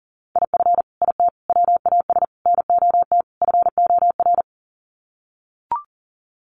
Morse code, heard as a keyed tone, about 37 wpm, 700 Hz. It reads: IF IT WAS NOT FOR
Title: Sets of 5 Words Encore 40wpm